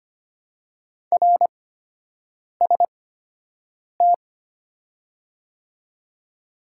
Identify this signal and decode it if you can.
Morse code, heard as a keyed tone, about 25 wpm, spaced as for 7 wpm, 700 Hz. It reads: RST
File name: Top 90 QSO Elements - 8x Character Spacing - 25wpm